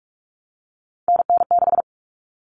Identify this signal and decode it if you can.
Morse code, heard as a keyed tone, about 44 wpm, 700 Hz. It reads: NN6
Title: Common Cut Numbers - 3-Digit 45wpm